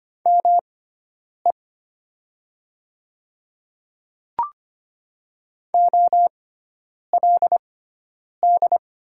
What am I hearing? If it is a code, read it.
Morse code, heard as a keyed tone, about 25 wpm, spaced as for 9 wpm, 700 Hz. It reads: ME OLD
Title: 6x Character Spacing - Top 200 Words - 25wpm